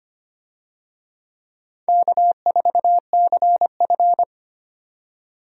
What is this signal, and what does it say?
Morse code, heard as a keyed tone, about 25 wpm, 700 Hz. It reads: K4CF
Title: USA Call Signs - 4 Characters - 25wpm